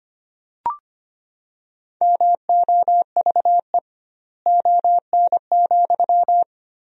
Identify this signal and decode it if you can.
Morse code, heard as a keyed tone, about 25 wpm, 700 Hz. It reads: MOVE ON,
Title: 2x Word-Spacing - Sentences from Top 500 Words - 25wpm